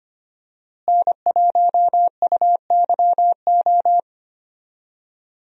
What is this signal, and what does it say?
Morse code, heard as a keyed tone, about 25 wpm, 700 Hz. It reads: N1UYO
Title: USA Call Signs - 5 Characters - Repeated 3x - 25wpm